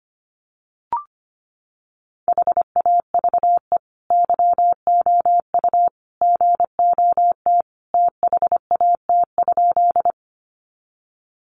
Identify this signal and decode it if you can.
Morse code, heard as a keyed tone, about 25 wpm, 700 Hz. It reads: HAVE YOU GOT THAT?
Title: Sentences from Top 300 Words 25wpm